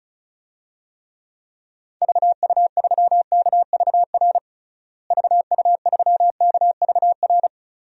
Morse code, heard as a keyed tone, about 34 wpm, 700 Hz. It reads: VU3KVR VU3KVR
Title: DXCC-100 Call Signs - 6 Characters - Repeated 3x - 35wpm